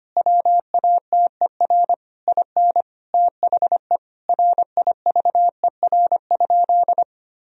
Morse code, heard as a keyed tone, about 25 wpm, 700 Hz. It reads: WATER IN THE RIVER?
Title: Sentences from Top 1000 Words - Review Part 4 - 25wpm